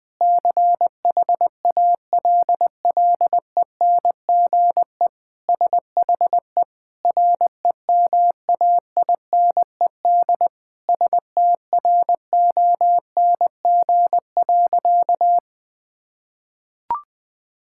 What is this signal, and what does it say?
Morse code, heard as a keyed tone, about 20 wpm, 700 Hz. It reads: CHALLENGE SHE REMAINED STRONG.
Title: Sentences with 5-Word Phrases - Spoken Beginning - 20wpm